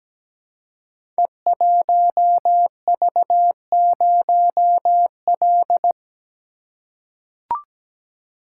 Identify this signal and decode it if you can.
Morse code, heard as a keyed tone, about 17 wpm, 700 Hz. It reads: E1V0L
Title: Code Groups - 5 Letter:Numbers - 17wpm